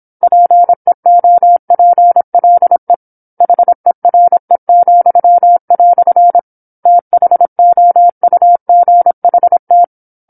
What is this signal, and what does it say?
Morse code, heard as a keyed tone, about 27 wpm, 700 Hz. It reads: PEOPLE HERE," THOUGHT
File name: 08-Chapter-08 26 WPM